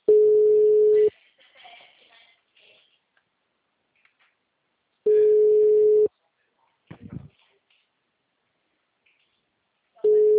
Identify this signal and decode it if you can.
Ringback tone (Europe, 425 Hz), with speech.